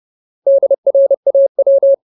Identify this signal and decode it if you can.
Morse code, heard as a keyed tone, about 30 wpm, 550 Hz. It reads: DRAW
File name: Ultimate Word List / 30WPM Ultimate